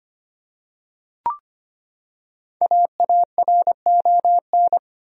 Morse code, heard as a keyed tone, about 25 wpm, 700 Hz. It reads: AARON
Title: USA Common Names - 25wpm